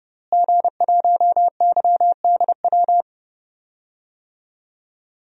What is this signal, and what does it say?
Morse code, heard as a keyed tone, about 30 wpm, 700 Hz. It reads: G1YDW
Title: DXCC-100 Call Signs - 5 Characters - Repeated 2x - 30wpm